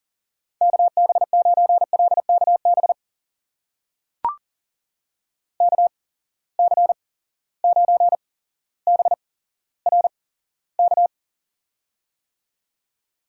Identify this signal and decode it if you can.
Morse code, heard as a keyed tone, about 41 wpm, spaced as for 17 wpm, 700 Hz. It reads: KB9LKB KC9BRK
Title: USA Call Signs - 6 Characters - 8x Character Spacing - 40wpm